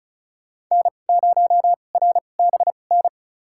Morse code, heard as a keyed tone, about 36 wpm, spaced as for 25 wpm, 700 Hz. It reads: N0RBN
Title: USA Call Signs - 2x Character Spacing - 35wpm